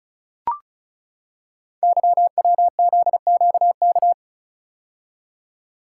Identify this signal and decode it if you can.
Morse code, heard as a keyed tone, about 35 wpm, 690 Hz. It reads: YWZQK
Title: Code Groups - 5 Letters - 35wpm